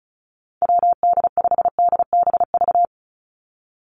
Morse code, heard as a keyed tone, about 34 wpm, 700 Hz. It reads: WD5DBV
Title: Call Signs Repeated 2x - 35wpm